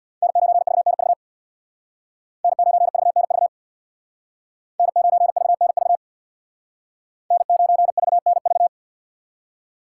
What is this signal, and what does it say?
Morse code, heard as a keyed tone, about 58 wpm, 700 Hz. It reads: N9VNV N9VNV N9VNV N9VNV
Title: Speed Racing - USA 1x3 Call Signs 50wpm